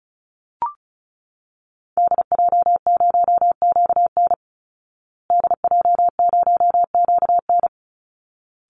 Morse code, heard as a keyed tone, about 34 wpm, 700 Hz. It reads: DJ0QN DJ0QN
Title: Call Signs Repeated 3x - 35wpm